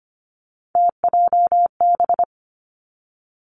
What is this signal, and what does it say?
Morse code, heard as a keyed tone, about 25 wpm, 700 Hz. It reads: TJB
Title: Three Challenging Letter-Number Combinations 25wpm